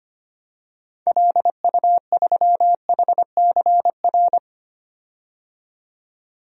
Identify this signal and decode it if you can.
Morse code, heard as a keyed tone, about 25 wpm, 700 Hz. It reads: LU3HCR
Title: DXCC-100 Call Signs - 6 Characters - 25wpm